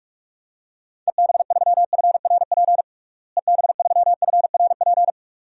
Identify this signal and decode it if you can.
Morse code, heard as a keyed tone, about 46 wpm, 700 Hz. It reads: EB3FRP EB3FRP
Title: DXCC-100 Call Signs - 6 Characters - Repeated 2x - 45wpm